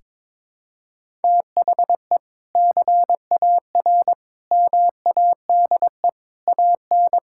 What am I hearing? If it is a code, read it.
Morse code, heard as a keyed tone, about 22 wpm, 700 Hz. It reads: THE CAR MADE AN
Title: Sentences from Top 1000 Words - Review Part 6 - 22wpm